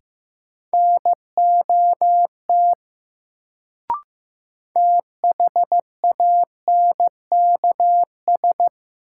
Morse code, heard as a keyed tone, about 15 wpm, 700 Hz. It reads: NOT THANKS
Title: Sentences from Top 600 Words - New Word Drill - 15wpm